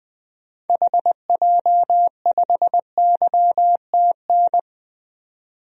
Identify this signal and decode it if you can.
Morse code, heard as a keyed tone, about 20 wpm, 700 Hz. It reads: HJ5YTN
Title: DXCC-100 Call Signs - 6 Characters - 20wpm